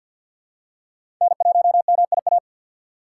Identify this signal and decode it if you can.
Morse code, heard as a keyed tone, about 50 wpm, 690 Hz. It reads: N1MIA
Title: USA Call Signs - 5 Characters - 4x Character Spacing - 50wpm